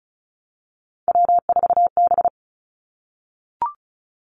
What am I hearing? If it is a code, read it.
Morse code, heard as a keyed tone, about 36 wpm, 700 Hz. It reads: W4B
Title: Three Challenging Letter-Number Combinations 35wpm